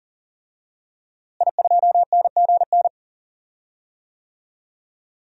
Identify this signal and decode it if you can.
Morse code, heard as a keyed tone, about 40 wpm, 700 Hz. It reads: I2NGN
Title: DXCC-100 Call Signs - 5 Characters - Repeated 2x - 40wpm